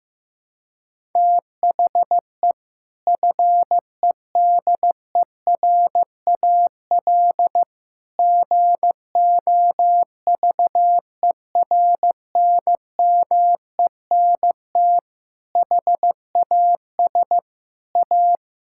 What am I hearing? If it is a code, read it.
Morse code, heard as a keyed tone, about 15 wpm, 700 Hz. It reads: THE FEDERAL GOVERNMENT HAS A
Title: Sentences from Top 1000 Words - Review Part 6 - 15wpm